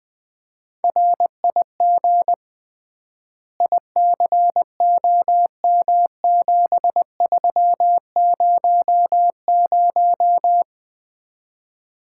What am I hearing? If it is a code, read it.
Morse code, heard as a keyed tone, about 20 wpm, 700 Hz. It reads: RIG ICOM7300
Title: Common QSO Phrases - 3x Word-Spacing - 20wpm